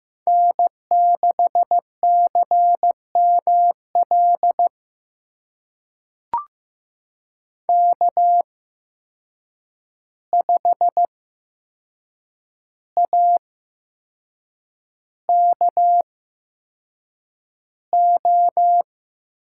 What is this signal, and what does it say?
Morse code, heard as a keyed tone, about 15 wpm, spaced as for 6 wpm, 700 Hz. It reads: N6CML K5AKO
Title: USA Call Signs - 8x Character Spacing - 15wpm